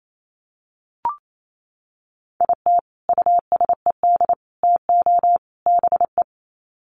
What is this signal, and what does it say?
Morse code, heard as a keyed tone, about 28 wpm, 700 Hz. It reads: IT USED TO BE
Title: Sets of 4 Words 28wpm